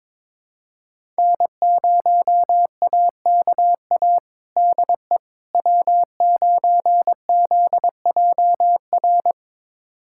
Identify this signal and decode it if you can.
Morse code, heard as a keyed tone, about 22 wpm, 700 Hz. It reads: N0AKA DE W9ZJR
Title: USA Callsign DE Callsign - 5 Characters - 22wpm